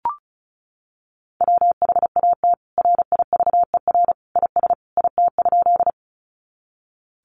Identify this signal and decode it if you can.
Morse code, heard as a keyed tone, about 34 wpm, 700 Hz. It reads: WHAT RIVER IS IT?
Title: Sentences from Top 300 Words 35wpm